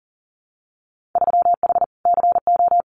Morse code, heard as a keyed tone, about 42 wpm, 700 Hz. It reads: 3H CO
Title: Winter Field Day Exchanges 40wpm